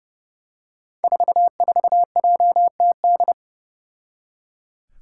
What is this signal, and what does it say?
Morse code, heard as a keyed tone, about 30 wpm, 700 Hz. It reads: 44JTD
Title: Five Challenging Letter-Number Combinations 30wpm